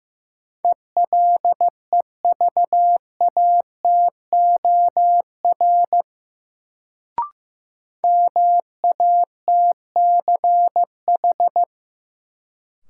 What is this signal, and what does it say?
Morse code, heard as a keyed tone, about 15 wpm, 700 Hz. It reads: ELEVATOR MATCH
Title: Sentences from Top 1600 Words - New Word Drill - 15wpm